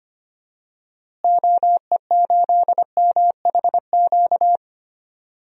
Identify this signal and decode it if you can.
Morse code, heard as a keyed tone, about 25 wpm, 700 Hz. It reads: OE8MHQ
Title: DXCC-100 Call Signs - 6 Characters - Repeated 2x - 25wpm